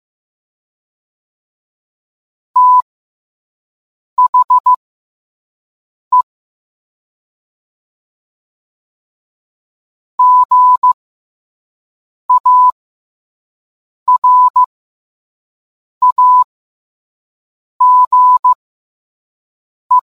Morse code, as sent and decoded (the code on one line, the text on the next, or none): - .... / --. .- .-. .- --. .
THE GARAGE